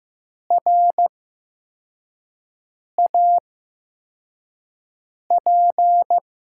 Morse code, heard as a keyed tone, about 15 wpm, spaced as for 4 wpm, 700 Hz. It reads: RAP
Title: USA Call Signs - 5 Characters - 8x Character Spacing - 15wpm